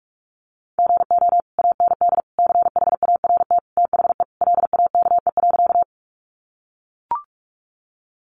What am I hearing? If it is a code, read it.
Morse code, heard as a keyed tone, about 44 wpm, 700 Hz. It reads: GO AND CHART THE LAKE.
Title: Sentences from Top 1300 Words 45wpm